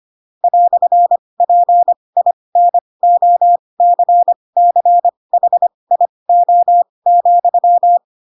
Morse code, heard as a keyed tone, about 25 wpm, spaced as for 20 wpm, 700 Hz. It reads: "PINOCCHIO,